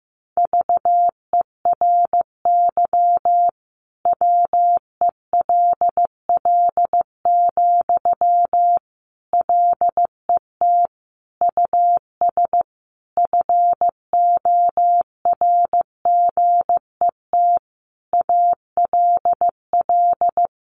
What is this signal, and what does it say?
Morse code, heard as a keyed tone, about 15 wpm, 700 Hz. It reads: VERY WELL, LET US FORGET ALL
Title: Sentences from Top 1200 Words 15wpm